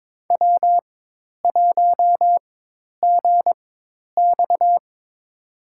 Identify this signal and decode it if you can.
Morse code, heard as a keyed tone, about 22 wpm, spaced as for 10 wpm, 700 Hz. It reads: W1GX
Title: USA Call Signs - 4 Characters - 4x Character Spacing - 22wpm